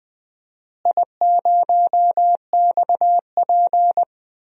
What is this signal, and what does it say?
Morse code, heard as a keyed tone, about 20 wpm, 700 Hz. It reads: I0XP
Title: DXCC-100 Call Signs - 4 Characters - Repeated 3x - 20wpm